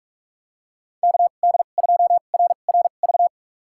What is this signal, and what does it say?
Morse code, heard as a keyed tone, about 46 wpm, spaced as for 33 wpm, 700 Hz. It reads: KD2RRV
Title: USA Call Signs - 2x Character Spacing - 45wpm